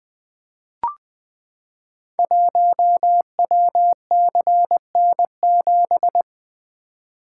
Morse code, heard as a keyed tone, about 20 wpm, 700 Hz. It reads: 1WCN7
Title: Code Groups - 5 Letter:Numbers - 20wpm